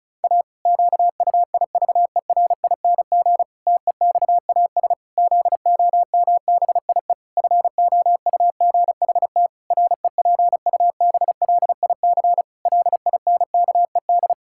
Morse code, heard as a keyed tone, about 34 wpm, 700 Hz. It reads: A QUIVERING TEXAS ZOMBIE FOUGHT REPUBLIC LINKED
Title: Pangram Challenge v2 - 35wpm